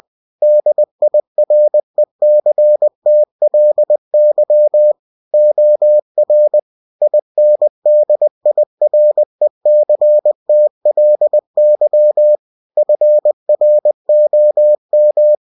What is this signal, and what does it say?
Morse code, heard as a keyed tone, about 20 wpm, 600 Hz. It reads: DIRECTLY OR INDIRECTLY FROM
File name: cw-books / winnie-the-pooh / winnie-20wpm-page-0120